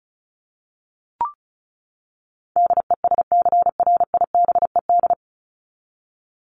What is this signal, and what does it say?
Morse code, heard as a keyed tone, about 34 wpm, 700 Hz. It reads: DESCRIBED
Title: ED Suffix - Top 100 Words with Base Words 35wpm